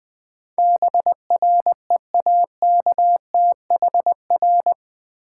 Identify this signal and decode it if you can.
Morse code, heard as a keyed tone, about 20 wpm, 700 Hz. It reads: BREAKTHR
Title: Build A Word - 4 Letters At A Time - 12-Letter Words - 20wpm